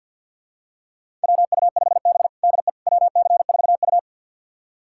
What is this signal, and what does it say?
Morse code, heard as a keyed tone, about 50 wpm, 700 Hz. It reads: WU5B DE WC4U
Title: USA Callsign DE Callsign - 4 Characters - 50wpm